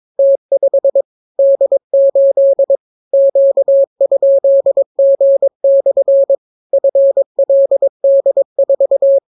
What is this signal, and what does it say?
Morse code, heard as a keyed tone, about 22 wpm, 550 Hz. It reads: T5 D8 Q?G/ FLD4